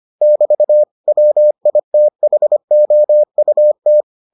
Morse code, heard as a keyed tone, about 25 wpm, 600 Hz. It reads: <BT> WITHOUT